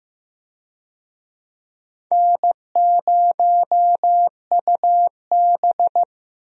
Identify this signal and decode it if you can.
Morse code, heard as a keyed tone, about 15 wpm, 700 Hz. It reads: N0UB
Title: USA Call Signs - 4 Characters - Repeated 2x - 15wpm